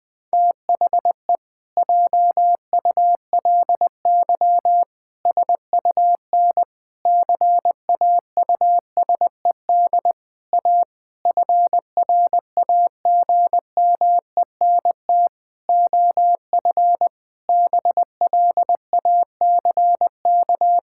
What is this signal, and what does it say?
Morse code, heard as a keyed tone, about 19 wpm, 700 Hz. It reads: THE JULY SUN CAUSED A FRAGMENT OF BLACK
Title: Pangram Challenge v2 - 20wpm